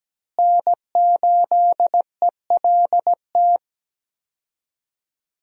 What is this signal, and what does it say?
Morse code, heard as a keyed tone, about 17 wpm, 700 Hz. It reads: N8ELT
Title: USA Call Signs - 5 Characters - Repeated 3x - 17wpm